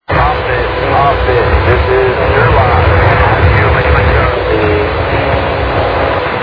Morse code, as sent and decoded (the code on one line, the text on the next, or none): none